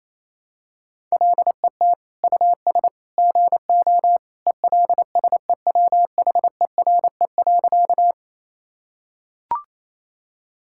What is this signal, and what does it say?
Morse code, heard as a keyed tone, about 28 wpm, 700 Hz. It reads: LET US GO ELSEWHERE.
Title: Sentences from Top 1600 Words 28wpm